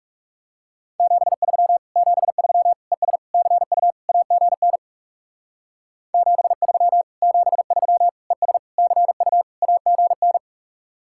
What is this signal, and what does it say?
Morse code, heard as a keyed tone, about 37 wpm, 700 Hz. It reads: 73 73 ES CU AGN 73 73 ES CU AGN
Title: Speed-Racing Common QSO Phrases - 30wpm